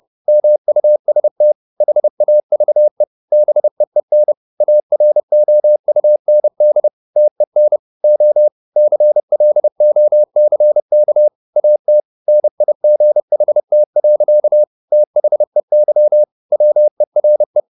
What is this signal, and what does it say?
Morse code, heard as a keyed tone, about 30 wpm, 600 Hz. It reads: MUST HAVE BEEN AROUND TEN O CLOCK AT NIGHT. THEY WERE